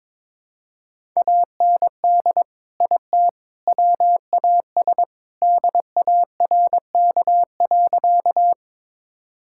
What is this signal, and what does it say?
Morse code, heard as a keyed tone, about 21 wpm, 700 Hz. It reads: AND IT WAS DARK.
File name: practice / 22wpm / Sentences with 3-Word Phrases - 22wpm